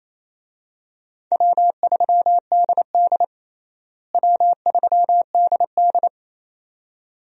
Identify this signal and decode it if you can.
Morse code, heard as a keyed tone, about 28 wpm, 700 Hz. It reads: W3DD W3DD
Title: USA Call Signs - 4 Characters - Repeated 2x - 28wpm